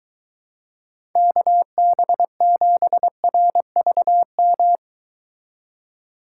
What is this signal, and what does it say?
Morse code, heard as a keyed tone, about 23 wpm, 700 Hz. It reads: KB7RVM